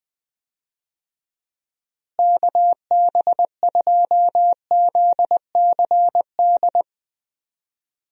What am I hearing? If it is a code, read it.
Morse code, heard as a keyed tone, about 21 wpm, 700 Hz. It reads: KB2ZCD
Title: USA Call Signs - 6 Characters - Repeated 2x - 20wpm